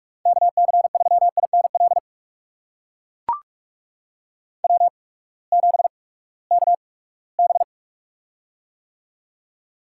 Morse code, heard as a keyed tone, about 42 wpm, spaced as for 22 wpm, 700 Hz. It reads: KC3INL W7KB
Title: USA Call Signs - 8x Character Spacing - 45wpm